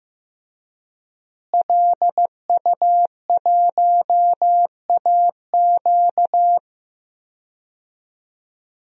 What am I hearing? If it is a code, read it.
Morse code, heard as a keyed tone, about 15 wpm, 700 Hz. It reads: LU1AQ
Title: DXCC-100 Call Signs - 5 Characters - Repeated 2x - 15wpm